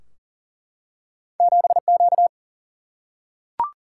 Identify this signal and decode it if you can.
Morse code, heard as a keyed tone, about 40 wpm, 700 Hz. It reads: 7Q
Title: Single-Letter Single-Number 40wpm